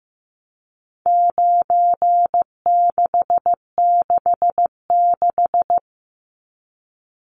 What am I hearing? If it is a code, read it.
Morse code, heard as a keyed tone, about 15 wpm, 700 Hz. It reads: 9666
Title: Numbers - 4-Digit 15wpm